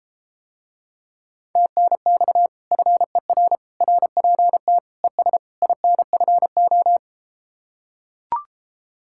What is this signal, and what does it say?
Morse code, heard as a keyed tone, about 33 wpm, 700 Hz. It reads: TNX FER RPT ES INFO